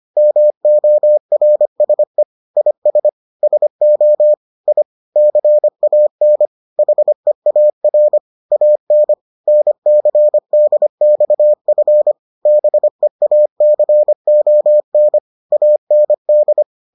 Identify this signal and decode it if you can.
Morse code, heard as a keyed tone, about 25 wpm, 600 Hz. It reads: MORSE IS SO I CAN HEAR AN NCDXF BEACON AND